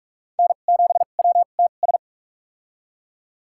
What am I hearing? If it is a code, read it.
Morse code, heard as a keyed tone, about 46 wpm, spaced as for 33 wpm, 700 Hz. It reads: N7WTS
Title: USA Call Signs - 2x Character Spacing - 45wpm